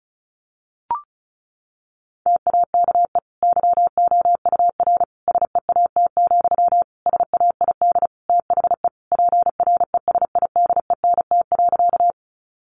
Morse code, heard as a keyed tone, about 34 wpm, 700 Hz. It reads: TAKE YOUR SEAT, SAID THE PRESIDENT.
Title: Sentences from Top 1500 Words 35wpm